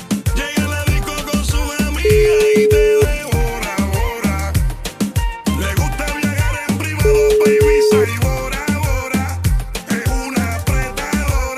Ringback tone (Europe, 425 Hz), with noise.